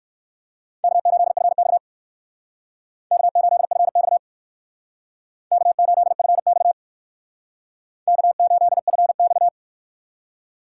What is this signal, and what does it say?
Morse code, heard as a keyed tone, about 59 wpm, 700 Hz. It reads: K8FX K8FX K8FX K8FX